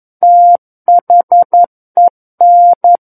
Morse code, heard as a keyed tone, about 11 wpm, 700 Hz. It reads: THEN
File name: Book One / 02-Chapter 2 - 11 WPM